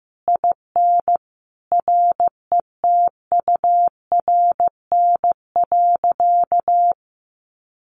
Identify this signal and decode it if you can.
Morse code, heard as a keyed tone, about 15 wpm, 700 Hz. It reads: IN RETURN.
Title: A Sound Of Thunder 15wpm